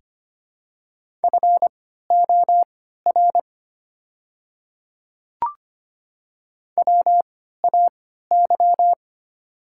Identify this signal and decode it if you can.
Morse code, heard as a keyed tone, about 25 wpm, spaced as for 14 wpm, 700 Hz. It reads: FOR WAY
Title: 3x Character Spacing - Top 100 Words - 25wpm